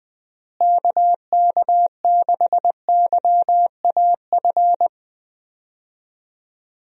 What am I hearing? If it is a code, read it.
Morse code, heard as a keyed tone, about 19 wpm, 700 Hz. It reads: KK6YAF
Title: USA Call Signs - 6 Characters - Repeated 3x - 20wpm